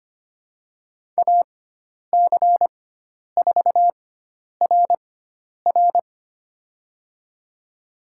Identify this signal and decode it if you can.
Morse code, heard as a keyed tone, about 25 wpm, spaced as for 10 wpm, 700 Hz. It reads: AC4RR
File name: Lesson 16g - Letter c - Call Signs 25wpm